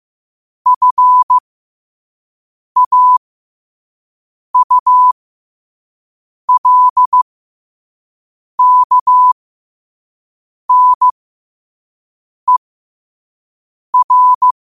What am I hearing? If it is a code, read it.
Morse code, heard as a keyed tone, about 15 wpm, spaced as for 5 wpm, 1000 Hz. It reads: FAULKNER
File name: qotdfiles / 5wpm